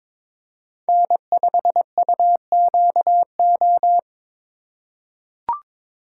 Morse code, heard as a keyed tone, about 22 wpm, 700 Hz. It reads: N5UQO
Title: USA Call Signs - 5 Characters - 4x Character Spacing - 22wpm